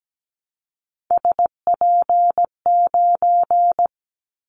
Morse code, heard as a keyed tone, about 17 wpm, 700 Hz. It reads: SP9
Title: Callsign Prefixes for Top 25 DXCC - 17wpm